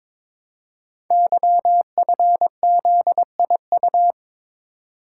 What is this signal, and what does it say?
Morse code, heard as a keyed tone, about 22 wpm, 700 Hz. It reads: YFZIU
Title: Code Groups - 5 Letters - 22wpm